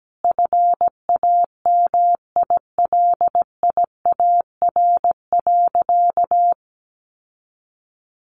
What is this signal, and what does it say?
Morse code, heard as a keyed tone, about 17 wpm, 700 Hz. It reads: FAMILIAR.
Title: Sentences from Top 1000 Words 17wpm